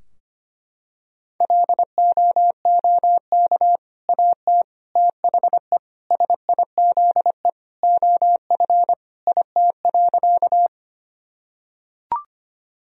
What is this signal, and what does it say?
Morse code, heard as a keyed tone, about 25 wpm, 700 Hz. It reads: LOOK AT THE SIZE OF IT.
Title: Sentences from Top 1000 Words - Review Part 2 - 25wpm